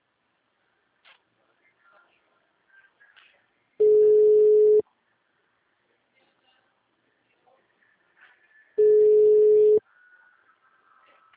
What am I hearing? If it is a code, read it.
Ringback tone (Europe, 425 Hz).